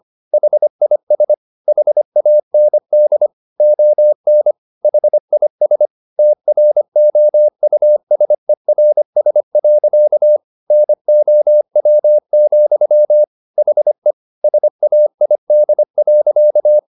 Morse code, heard as a keyed tone, about 25 wpm, 600 Hz. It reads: HIS HAND ON HIS TROUSERS. NOW, HE SAID.
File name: the old man and the sea part 1 25wpm